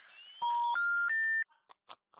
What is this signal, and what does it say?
Special information tone.
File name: out- -2025